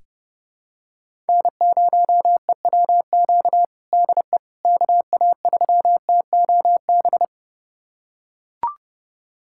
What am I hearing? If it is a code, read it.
Morse code, heard as a keyed tone, about 30 wpm, 700 Hz. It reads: N0EWQ DE KA3TOB
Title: USA Callsign DE Callsign 30wpm